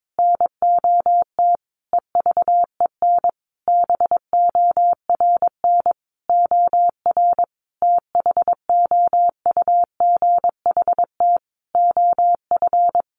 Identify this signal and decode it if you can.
Morse code, heard as a keyed tone, about 22 wpm, 700 Hz. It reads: NOT EVEN BORN OR THOUGHT OF
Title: A Sound Of Thunder 22wpm